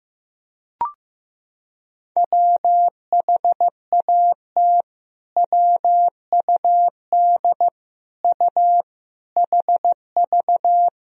Morse code, heard as a keyed tone, about 15 wpm, 700 Hz. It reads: WHAT WUD U HV
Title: Sentences from Top 300 Words and Abbreviations 15wpm